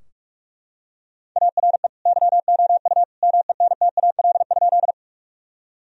Morse code, heard as a keyed tone, about 43 wpm, 700 Hz. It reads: ARE YOU MENTAL?
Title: Sentences from Top 1000 Words - Review Part 4 - 45wpm